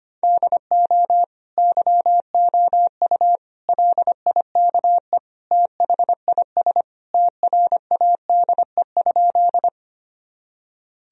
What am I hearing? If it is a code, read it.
Morse code, heard as a keyed tone, about 25 wpm, 690 Hz. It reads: DO YOU LIKE THIS TRADE?
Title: Sentences from Top 1000 Words - Review Part 4 - 25wpm